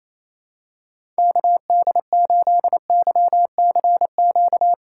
Morse code, heard as a keyed tone, about 28 wpm, 700 Hz. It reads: KD8YCQ
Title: USA Call Signs - 6 Characters - 4x Character Spacing - 28wpm